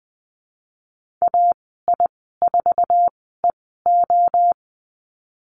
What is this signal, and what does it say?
Morse code, heard as a keyed tone, about 20 wpm, spaced as for 15 wpm, 700 Hz. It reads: AI4EO
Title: 062 Lesson 11g - Letter h - Call Signs 20wpm